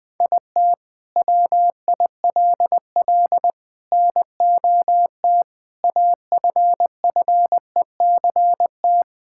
Morse code, heard as a keyed tone, about 21 wpm, 700 Hz. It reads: IT WILL NOT AFFECT
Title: Sentences from Top 1000 Words - Review Part 6 - 20wpm